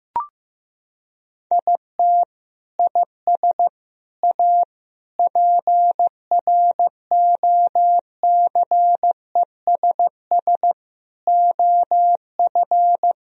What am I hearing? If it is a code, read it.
Morse code, heard as a keyed tone, about 14 wpm, 700 Hz. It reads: IT IS A PROCESS OF
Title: Sentences from Top 1000 Words - Review Part 4 - 15wpm